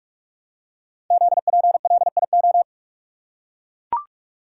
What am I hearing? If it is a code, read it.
Morse code, heard as a keyed tone, about 44 wpm, 700 Hz. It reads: ZPLIO
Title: Code Groups - 5 Letters - 45wpm